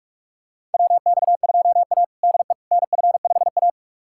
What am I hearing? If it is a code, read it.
Morse code, heard as a keyed tone, about 41 wpm, 700 Hz. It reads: WX2A DE NF5A
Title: USA Callsign DE Callsign - 4 Characters - 45wpm